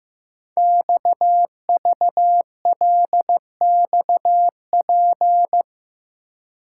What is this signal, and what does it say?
Morse code, heard as a keyed tone, about 15 wpm, 700 Hz. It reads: XVLXP
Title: Code Groups - 5 Letters - 15wpm